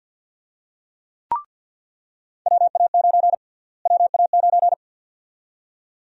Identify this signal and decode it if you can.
Morse code, heard as a keyed tone, about 50 wpm, 700 Hz. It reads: WA9 WA9
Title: USA Call Sign Prefixes - Repeated 2x - 50wpm